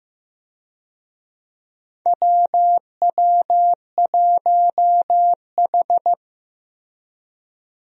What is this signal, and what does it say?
Morse code, heard as a keyed tone, about 15 wpm, 700 Hz. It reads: WW1H
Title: USA Call Signs - 4 Characters - Repeated 3x - 15wpm